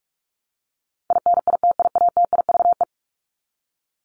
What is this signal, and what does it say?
Morse code, heard as a keyed tone, about 44 wpm, 700 Hz. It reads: INITIATIVE
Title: Top 1751-2000 Words 45wpm